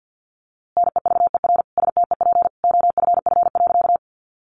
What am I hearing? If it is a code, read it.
Morse code, heard as a keyed tone, about 50 wpm, 700 Hz. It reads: NEVER STEP OFF.
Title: A Sound Of Thunder 50wpm